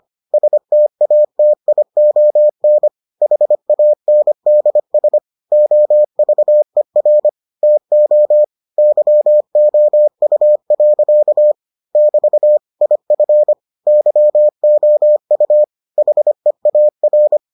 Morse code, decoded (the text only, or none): STATION HANDS OVER TO YOU. <BT> IF YOU HEAR